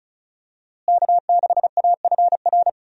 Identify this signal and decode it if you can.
Morse code, heard as a keyed tone, about 36 wpm, 700 Hz. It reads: K6AFR
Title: USA Call Signs - 5 Characters - Repeated 3x - 35wpm